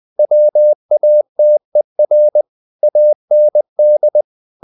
Morse code, heard as a keyed tone, about 20 wpm, 600 Hz. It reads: WATER AND
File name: the old man and the sea part 2 20wpm